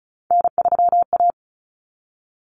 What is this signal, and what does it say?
Morse code, heard as a keyed tone, about 34 wpm, 700 Hz. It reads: N3A
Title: Common Cut Numbers - 3-Digit 35wpm